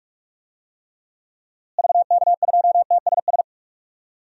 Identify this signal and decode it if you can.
Morse code, heard as a keyed tone, about 43 wpm, 700 Hz. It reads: VK2TSS
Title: DXCC-100 Call Signs - 6 Characters - Repeated 2x - 45wpm